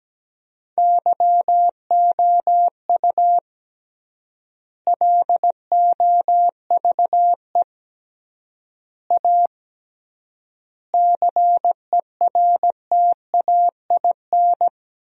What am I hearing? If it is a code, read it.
Morse code, heard as a keyed tone, about 17 wpm, 700 Hz. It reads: YOU LOVE A CERTAIN
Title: 3x Word-Spacing - Sentences from Top 500 Words - 17wpm